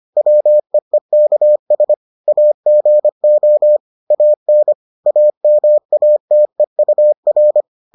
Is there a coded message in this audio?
Yes: Morse code from 0.2 to 7.6 s, heard as a keyed tone, about 25 wpm, 600 Hz. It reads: WEEKS AGO AN AMATEUR